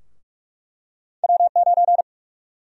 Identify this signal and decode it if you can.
Morse code, heard as a keyed tone, about 46 wpm, 700 Hz. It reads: W9
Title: USA Call Sign Prefixes - 45wpm